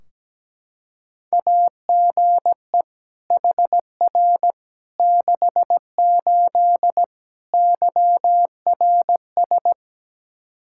Morse code, heard as a keyed tone, about 17 wpm, 700 Hz. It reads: AGE HR 68 YRS
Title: Practice QSO - Exchange 3 - More Info - 17wpm